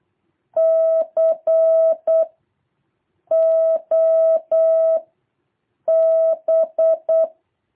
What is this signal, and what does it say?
Morse code, heard as a keyed tone, about 8 wpm, 650 Hz. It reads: C O B
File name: jh8cbh / 2023cw